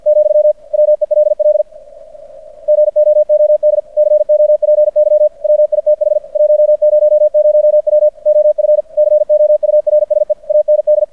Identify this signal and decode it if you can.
Morse code, heard as a keyed tone, about 50 wpm, 600 Hz. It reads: , WELL GOOD COPY WITH 100W OF COURSE AND
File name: archived / CRSnet